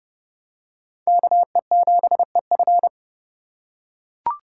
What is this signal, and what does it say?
Morse code, heard as a keyed tone, about 27 wpm, 700 Hz. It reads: KE7EF